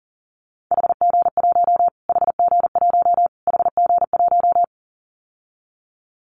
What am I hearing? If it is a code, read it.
Morse code, heard as a keyed tone, about 40 wpm, 700 Hz. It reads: HG1 HG1 HG1
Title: Callsign Prefixes for Top 25 DXCC - Repeated 3x - 40wpm